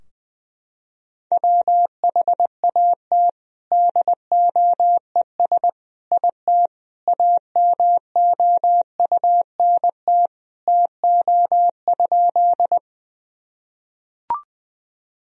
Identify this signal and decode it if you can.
Morse code, heard as a keyed tone, about 20 wpm, 700 Hz. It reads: WHAT DOES IT AMOUNT TO?
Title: Sentences from Top 1000 Words - Review Part 4 - 20wpm